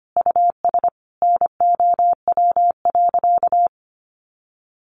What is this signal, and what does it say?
Morse code, heard as a keyed tone, about 25 wpm, 700 Hz. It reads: US NOW.
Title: Sentences from Top 1400 Words 25wpm